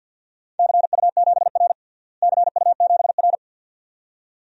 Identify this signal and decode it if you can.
Morse code, heard as a keyed tone, about 49 wpm, 700 Hz. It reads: CU7R CU7R
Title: DXCC-100 Call Signs - 4 Characters - Repeated 2x - 50wpm